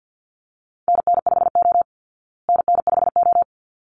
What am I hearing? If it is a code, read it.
Morse code, heard as a keyed tone, about 48 wpm, 700 Hz. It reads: NN5O NN5O
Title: Call Signs Repeated 3x - 50wpm